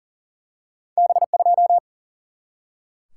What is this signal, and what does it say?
Morse code, heard as a keyed tone, about 40 wpm, 700 Hz. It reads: B2
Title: Single-Letter Single-Number 40wpm